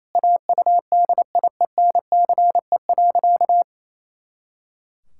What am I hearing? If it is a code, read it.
Morse code, heard as a keyed tone, about 28 wpm, 700 Hz. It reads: AUDIENCE.
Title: Pangram Challenge v2 - 28wpm